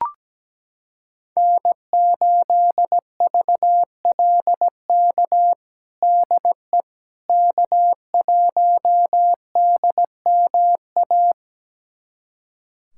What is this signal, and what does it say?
Morse code, heard as a keyed tone, about 17 wpm, 700 Hz. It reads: N8VLK DE K1DMA
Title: USA Callsign DE Callsign - 5 Characters - 17wpm